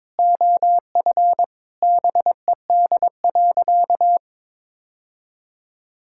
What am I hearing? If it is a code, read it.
Morse code, heard as a keyed tone, about 22 wpm, 700 Hz. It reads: OF BED.
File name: practice / 22wpm / Sentences with 3-Word Phrases - 22wpm